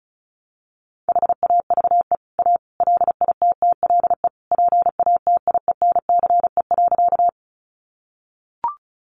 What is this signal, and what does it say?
Morse code, heard as a keyed tone, about 34 wpm, 700 Hz. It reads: HAVE A LITTLE PATIENCE.
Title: Sentences from Top 1300 Words 35wpm